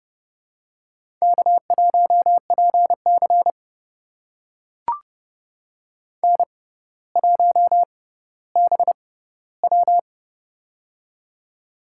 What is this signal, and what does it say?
Morse code, heard as a keyed tone, about 28 wpm, spaced as for 15 wpm, 700 Hz. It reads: K1PC N1BW
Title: USA Call Signs - 4 Characters - 6x Character Spacing - 30wpm